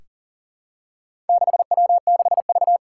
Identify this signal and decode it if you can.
Morse code, heard as a keyed tone, about 42 wpm, 700 Hz. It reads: 6W6V
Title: Four Challenging Letter-Number Combinations 40wpm